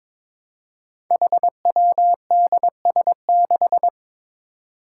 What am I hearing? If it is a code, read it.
Morse code, heard as a keyed tone, about 22 wpm, 700 Hz. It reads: HWDS6
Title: Five Challenging Letter-Number Combinations 22wpm